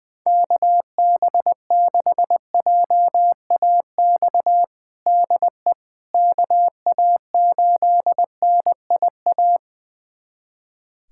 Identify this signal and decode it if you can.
Morse code, heard as a keyed tone, about 20 wpm, 700 Hz. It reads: KB6JAX DE KA8NIA
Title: USA Callsign DE Callsign - 6 Characters - 20wpm